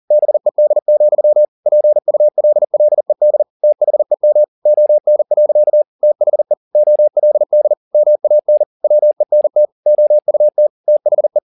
Morse code, heard as a keyed tone, about 40 wpm, 600 Hz. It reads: BED, PULLED THEM ON. THE OLD MAN WENT OUT THE